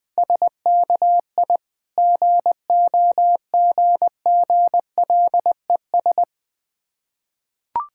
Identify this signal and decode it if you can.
Morse code, heard as a keyed tone, about 20 wpm, 690 Hz. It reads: SKI GOGGLES
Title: Difficult Words with Hints - 20wpm